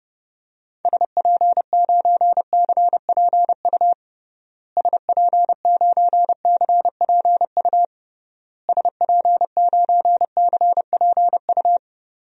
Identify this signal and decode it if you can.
Morse code, heard as a keyed tone, about 30 wpm, 700 Hz. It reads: SP9CPU SP9CPU SP9CPU
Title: DXCC-100 Call Signs - 6 Characters - Repeated 3x - 30wpm